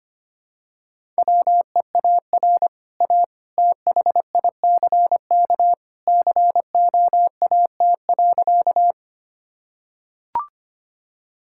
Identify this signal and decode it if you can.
Morse code, heard as a keyed tone, about 25 wpm, 690 Hz. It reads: WEAR A THICK COAT.
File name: Easy Sentences with Hints - 25wpm